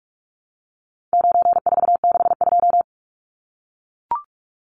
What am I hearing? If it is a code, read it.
Morse code, heard as a keyed tone, about 44 wpm, 700 Hz. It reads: 9462
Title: Numbers - 4-Digit 45wpm